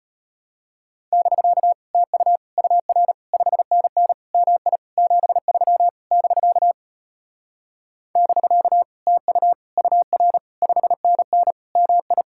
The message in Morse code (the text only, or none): <BK> TU UR 5NN MI 73 <BK> <BK> TU UR 5NN MI